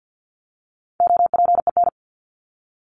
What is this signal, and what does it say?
Morse code, heard as a keyed tone, about 50 wpm, 700 Hz. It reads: OPEN
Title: ING Suffix - Top 100 Words with Base Words 50wpm